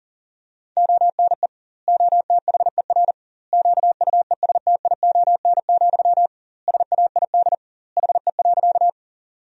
Morse code, heard as a keyed tone, about 40 wpm, 700 Hz. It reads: ONE OTHER QUESTION, SAID HE.
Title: 2x Word-Spacing - Sentences from Top 400 Words - 40wpm